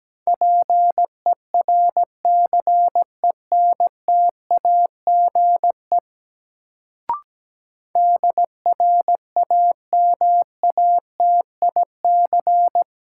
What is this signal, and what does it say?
Morse code, heard as a keyed tone, about 17 wpm, 700 Hz. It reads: PERCENTAGE DRAMATIC
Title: Sentences from Top 1300 Words - New Word Drill - 17wpm